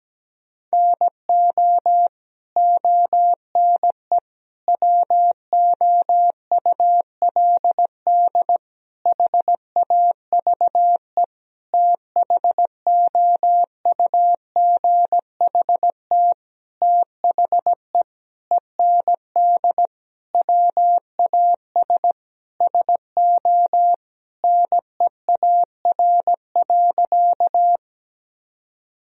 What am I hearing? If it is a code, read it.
Morse code, heard as a keyed tone, about 17 wpm, 700 Hz. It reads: NO ONE WOULD HAVE THOUGHT THE END WAS SO NEAR.
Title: Sentences from Top 1000 Words - Review Part 1 - 17wpm